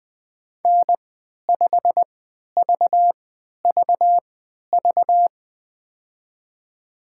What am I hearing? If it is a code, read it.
Morse code, heard as a keyed tone, about 20 wpm, spaced as for 11 wpm, 700 Hz. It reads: N5VVV
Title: USA Call Signs - 5 Characters - 3x Character Spacing - 20wpm